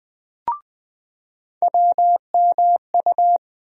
Morse code, heard as a keyed tone, about 20 wpm, 700 Hz. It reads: WMU
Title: USA Call Sign Postfixes - 3 Letters 20wpm